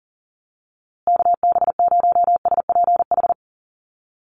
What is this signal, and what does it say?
Morse code, heard as a keyed tone, about 40 wpm, 700 Hz. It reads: KB0SPH
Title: USA Call Signs Repeated 2x 40wpm